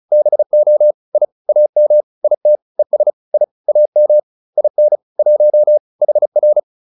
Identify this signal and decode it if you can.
Morse code, heard as a keyed tone, about 34 wpm, 600 Hz. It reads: BO I AM IT ES I AM IN 1 HR